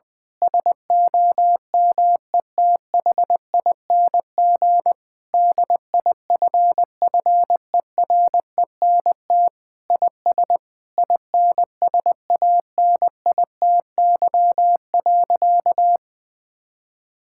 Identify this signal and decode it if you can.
Morse code, heard as a keyed tone, about 20 wpm, 700 Hz. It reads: SOMETHING DIFFERENT IS INSANITY.
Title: Sentences with 8-Word Phrases - Spoken Beginning - 20wpm